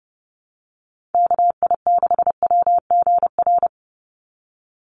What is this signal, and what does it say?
Morse code, heard as a keyed tone, about 30 wpm, 700 Hz. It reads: KI6WGR
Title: USA Call Signs 30wpm